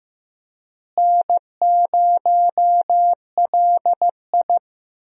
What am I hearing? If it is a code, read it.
Morse code, heard as a keyed tone, about 15 wpm, 700 Hz. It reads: N0LI
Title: USA Call Signs - 4 Characters - Repeated 3x - 15wpm